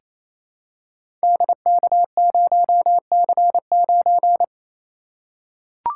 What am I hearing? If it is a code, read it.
Morse code, heard as a keyed tone, about 28 wpm, 700 Hz. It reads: DK0C9
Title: Code Groups - 5 Letter:Numbers - 28wpm